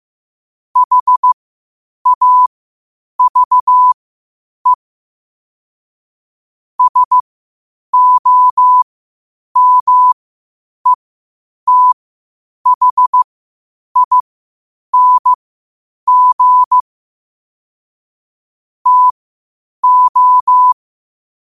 Morse code, heard as a keyed tone, about 15 wpm, spaced as for 8 wpm, 1000 Hz. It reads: HAVE SOMETHING TO
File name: qotdfiles / 8wpm